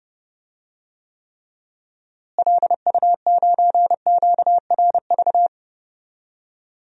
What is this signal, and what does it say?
Morse code, heard as a keyed tone, about 32 wpm, 700 Hz. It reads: LU9QRV
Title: DXCC-100 Call Signs - 6 Characters - Repeated 2x - 30wpm